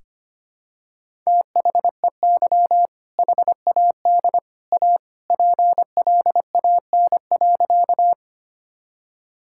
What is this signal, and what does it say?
Morse code, heard as a keyed tone, about 25 wpm, 700 Hz. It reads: THEY HAD A PLAN.
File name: practice / 25wpm / Sentences with 2-Word Phrases - 25wpm